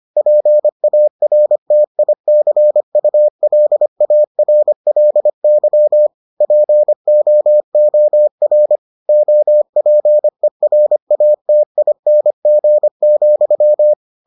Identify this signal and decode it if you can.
Morse code, heard as a keyed tone, about 25 wpm, 600 Hz. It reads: PARTICULARLY POOR OPERATING,